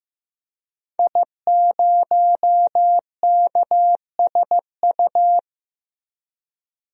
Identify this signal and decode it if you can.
Morse code, heard as a keyed tone, about 15 wpm, 700 Hz. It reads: I0KSU
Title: DXCC-100 Call Signs - 5 Characters - 15wpm